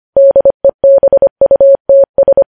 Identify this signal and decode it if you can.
Morse code, heard as a keyed tone, about 25 wpm, 550 Hz. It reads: DEBUTS
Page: news-headlines-in-morse-code-at-25-wpm